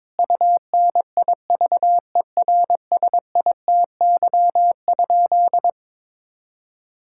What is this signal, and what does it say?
Morse code, heard as a keyed tone, about 22 wpm, 700 Hz. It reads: UNIVERSITY?
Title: Sentences from Top 1000 Words - Review Part 5 - 22wpm